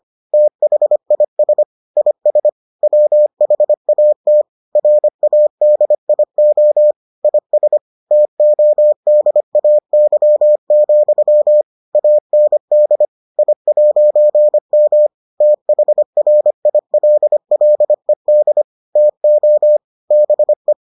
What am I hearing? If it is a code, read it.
Morse code, heard as a keyed tone, about 25 wpm, 600 Hz. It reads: THIS IS WHAT RADIO IS TODAY, AND I'M THRILLED TO BE